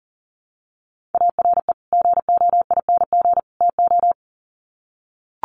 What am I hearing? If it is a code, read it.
Morse code, heard as a keyed tone, about 40 wpm, 700 Hz. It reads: ARE GOING TO
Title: Sets of 3 Words 40wpm